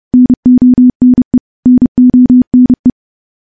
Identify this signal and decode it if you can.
Morse code, heard as a keyed tone, about 30 wpm, 250 Hz. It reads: NONE NONE
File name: morse / morse-training-30WPM